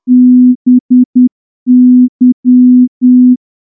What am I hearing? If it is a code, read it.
Morse code, heard as a keyed tone, about 9 wpm, 250 Hz. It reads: BY
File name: morse-training-10WPM